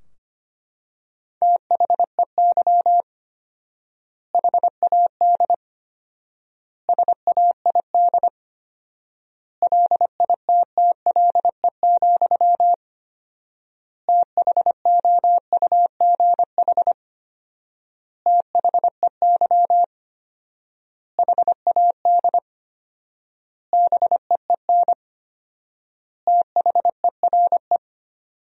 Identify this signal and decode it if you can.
Morse code, heard as a keyed tone, about 25 wpm, 700 Hz. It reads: THEY HAD SAID LITTLE, THOUGH THEY HAD BEEN THERE
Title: 4x Word-Spacing - Sentences from Top 400 Words - 25wpm